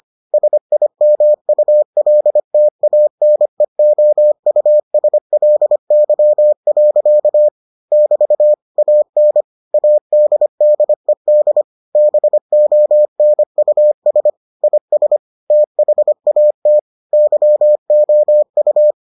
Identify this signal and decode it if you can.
Morse code, heard as a keyed tone, about 25 wpm, 600 Hz. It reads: SIMULTANEOUSLY. <BT> AN ADDED BONUS IS THAT YOU